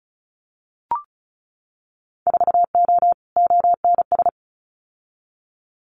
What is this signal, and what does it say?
Morse code, heard as a keyed tone, about 34 wpm, 700 Hz. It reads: 4O ONS
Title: Winter Field Day Exchanges 35wpm